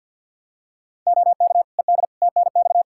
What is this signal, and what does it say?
Morse code, heard as a keyed tone, about 46 wpm, 700 Hz. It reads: OK ED TNX
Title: K1USN SST QSO Practice 50wpm